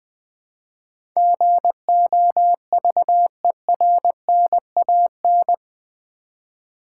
Morse code, heard as a keyed tone, about 20 wpm, 700 Hz. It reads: GOVERNAN
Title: Build A Word - 4 Letters At A Time - 10-Letter Words - 20wpm